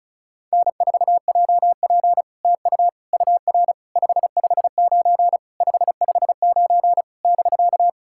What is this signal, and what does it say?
Morse code, heard as a keyed tone, about 36 wpm, 700 Hz. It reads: N4JP TU UR 559 559 <BK>
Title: SOTA QSO Practice 35wpm